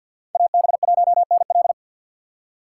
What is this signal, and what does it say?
Morse code, heard as a keyed tone, about 53 wpm, 700 Hz. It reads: AB1NL
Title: USA Call Signs - 6x Character Spacing - 50wpm